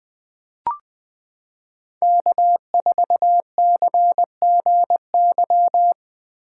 Morse code, heard as a keyed tone, about 20 wpm, 700 Hz. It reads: K4CGY
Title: 2023 POTA Hunter Call Signs - 20wpm